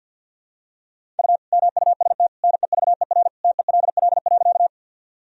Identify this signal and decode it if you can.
Morse code, heard as a keyed tone, about 46 wpm, 700 Hz. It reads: U MUST NEVER TELL.
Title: Sentences from Top 300 Words and Abbreviations 50wpm